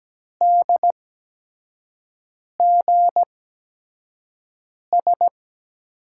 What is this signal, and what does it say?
Morse code, heard as a keyed tone, about 17 wpm, spaced as for 5 wpm, 700 Hz. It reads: DGS
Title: USA Call Signs - 8x Character Spacing - 17wpm